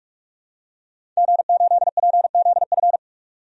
Morse code, heard as a keyed tone, about 46 wpm, 700 Hz. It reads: G8PZF
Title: DXCC-100 Call Signs - 5 Characters - 45wpm